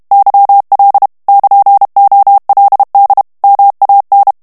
Morse code, heard as a keyed tone, about 32 wpm, 800 Hz. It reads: YL <KN>OLD MAN
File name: ITALIEN texte clair 1 vitesse 30 mots minute